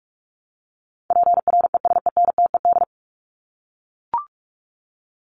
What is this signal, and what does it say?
Morse code, heard as a keyed tone, about 44 wpm, 700 Hz. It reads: PRESENTED
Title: ED Suffix - Top 100 Words with Base Words 45wpm